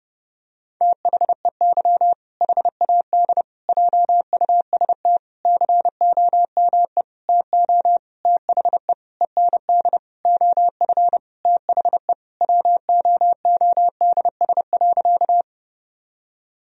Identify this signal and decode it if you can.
Morse code, heard as a keyed tone, about 30 wpm, 700 Hz. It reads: THEY HAD JUST COME TO THE END OF THE WOODS.
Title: Sentences from Top 1000 Words - Review Part 2 - 30wpm